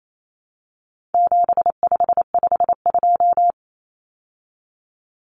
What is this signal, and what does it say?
Morse code, heard as a keyed tone, about 28 wpm, 700 Hz. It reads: 7552
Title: Common Cut Numbers - 4-Digit 28wpm